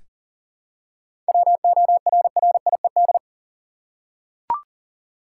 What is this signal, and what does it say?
Morse code, heard as a keyed tone, about 40 wpm, 700 Hz. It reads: WORRIED
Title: Seven Letter Words - 40wpm